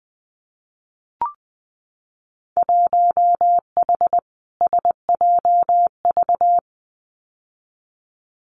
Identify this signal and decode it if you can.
Morse code, heard as a keyed tone, about 20 wpm, 700 Hz. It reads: 1H SJV
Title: Winter Field Day Exchanges 20wpm